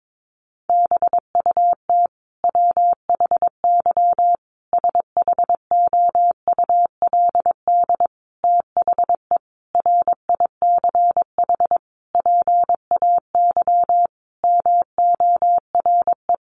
Morse code, heard as a keyed tone, about 22 wpm, 700 Hz. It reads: BUT WHY SHOULD THE RICH PAY MORE
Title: Sentences from Top 1100 Words 22wpm